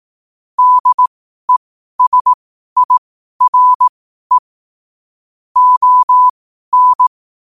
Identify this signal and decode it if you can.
Morse code, heard as a keyed tone, about 18 wpm, spaced as for 12 wpm, 1000 Hz. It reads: DESIRE ON
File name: qotdfiles / 12wpm